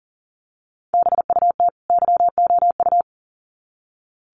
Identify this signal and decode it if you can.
Morse code, heard as a keyed tone, about 37 wpm, 700 Hz. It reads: BUT YOU
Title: Sets of 2 Words 40wpm